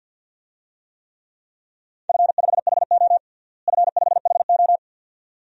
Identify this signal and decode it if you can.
Morse code, heard as a keyed tone, about 52 wpm, 700 Hz. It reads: F5HO F5HO
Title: DXCC-100 Call Signs - 4 Characters - Repeated 2x - 50wpm